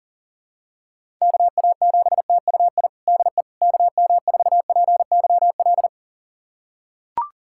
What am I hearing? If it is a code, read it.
Morse code, heard as a keyed tone, about 40 wpm, 700 Hz. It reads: KA7TUI DE KM4PYL
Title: USA Callsign DE Callsign 40wpm